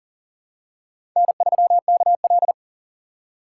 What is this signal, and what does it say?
Morse code, heard as a keyed tone, about 38 wpm, 700 Hz. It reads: N3KL
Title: USA Call Signs - 4 Characters - 2x Character Spacing - 40wpm